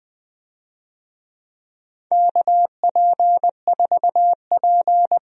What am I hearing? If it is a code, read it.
Morse code, heard as a keyed tone, about 20 wpm, 700 Hz. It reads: KP4P
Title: USA Call Signs - 4 Characters - 20wpm